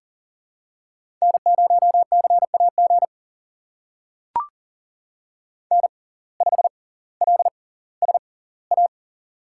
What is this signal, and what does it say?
Morse code, heard as a keyed tone, about 40 wpm, spaced as for 20 wpm, 700 Hz. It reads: N0CAG N5LSA
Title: USA Call Signs - 5 Characters - 6x Character Spacing - 40wpm